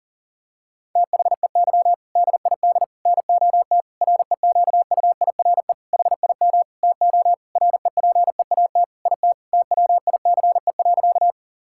Morse code, heard as a keyed tone, about 40 wpm, 700 Hz. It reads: THEY DID NOT REQUIRE HIM TO REPEAT IT TWICE.
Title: Sentences from Top 1600 Words 40wpm